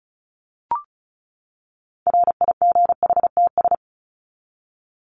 Morse code, heard as a keyed tone, about 34 wpm, 700 Hz. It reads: RIGHTS
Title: ES:S Suffix - Top 100 Words with Base Words 35wpm